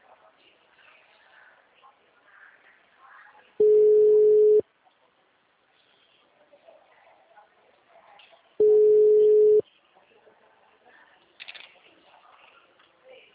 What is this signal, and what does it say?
Ringback tone (Europe, 425 Hz).
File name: out- -2003- - - .70738